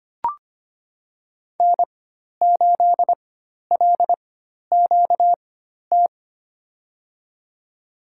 Morse code, heard as a keyed tone, about 25 wpm, spaced as for 12 wpm, 700 Hz. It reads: N8LQT
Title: USA Call Signs - 5 Characters - 4x Character Spacing - 25wpm